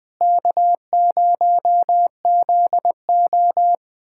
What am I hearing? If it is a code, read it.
Morse code, heard as a keyed tone, about 20 wpm, 700 Hz. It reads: K0ZO
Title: USA Callsign DE Callsign - 4 Characters - 20wpm